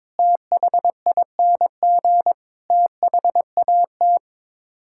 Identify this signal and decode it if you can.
Morse code, heard as a keyed tone, about 21 wpm, 700 Hz. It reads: THING THAT
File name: practice / 22wpm / Sets of 7 Words - 22wpm